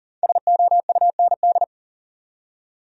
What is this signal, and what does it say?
Morse code, heard as a keyed tone, about 50 wpm, 700 Hz. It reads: SOUND
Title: Speed-Racing Top 500 Words 40wpm